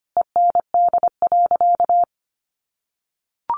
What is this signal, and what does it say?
Morse code, heard as a keyed tone, about 25 wpm, 700 Hz. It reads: END.
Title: Sentences from Top 1300 Words 25wpm